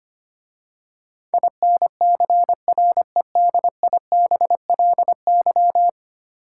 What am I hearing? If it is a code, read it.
Morse code, heard as a keyed tone, about 25 wpm, 700 Hz. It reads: INCREDIBLY
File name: 10 Letter Words - First 3 Letters Given - 25wpm